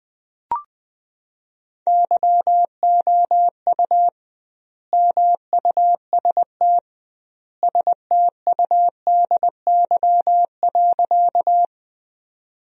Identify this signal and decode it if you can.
Morse code, heard as a keyed tone, about 20 wpm, 700 Hz. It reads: YOU MUST STUDY.
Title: 2x Word-Spacing - Sentences from Top 300 Words - 20wpm